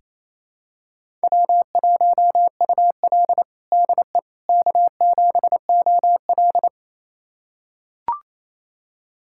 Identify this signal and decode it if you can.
Morse code, heard as a keyed tone, about 28 wpm, 700 Hz. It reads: W1UL DE K7OL
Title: USA Callsign DE Callsign - 4 Characters - 28wpm